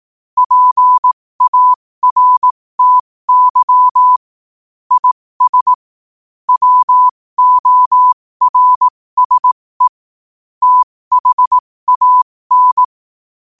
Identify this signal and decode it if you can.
Morse code, heard as a keyed tone, about 18 wpm, spaced as for 15 wpm, 1000 Hz. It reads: PARTY IS WORSE THAN
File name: qotdfiles / 15wpm